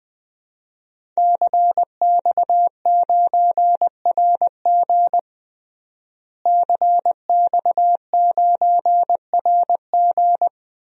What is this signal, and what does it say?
Morse code, heard as a keyed tone, about 21 wpm, 700 Hz. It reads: CX9RG CX9RG
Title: DXCC-100 Call Signs - 5 Characters - Repeated 2x - 20wpm